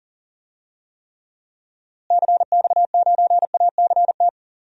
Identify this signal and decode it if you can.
Morse code, heard as a keyed tone, about 37 wpm, 700 Hz. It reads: CX9ACT
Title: DXCC-100 Call Signs - 6 Characters - Repeated 2x - 40wpm